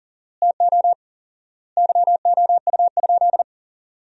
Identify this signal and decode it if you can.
Morse code, heard as a keyed tone, about 40 wpm, 700 Hz. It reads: TO YOU?
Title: 4x Word-Spacing - Sentences from Top 100 Words - 40wpm